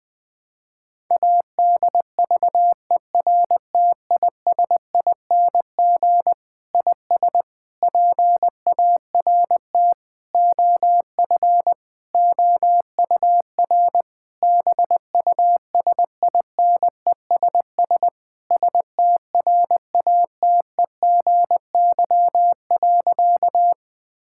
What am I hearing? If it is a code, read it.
Morse code, heard as a keyed tone, about 20 wpm, 700 Hz. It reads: ADVERTISING IS PART OF OUR BUSINESS STRATEGY.
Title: Sentences from Top 1000 Words - Review Part 6 - 20wpm